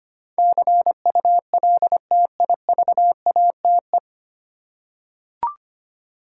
Morse code, heard as a keyed tone, about 25 wpm, 700 Hz. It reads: CULTIVATE
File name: practice / 25wpm / 9 Letter Words - First 3 Letters Given - 25wpm